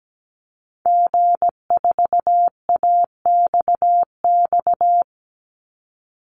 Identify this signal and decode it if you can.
Morse code, heard as a keyed tone, about 17 wpm, 700 Hz. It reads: G4AXX
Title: Call Signs - 17wpm